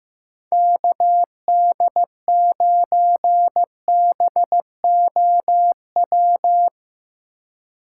Morse code, heard as a keyed tone, about 15 wpm, 700 Hz. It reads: KD9BOW
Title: USA Call Signs - 2x Character Spacing - 15wpm